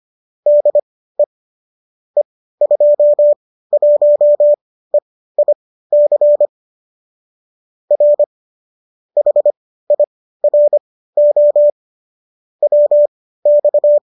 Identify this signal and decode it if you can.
Morse code, heard as a keyed tone, about 25 wpm, spaced as for 15 wpm, 590 Hz. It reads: DE E21EIC R HIRO WX
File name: qso208 15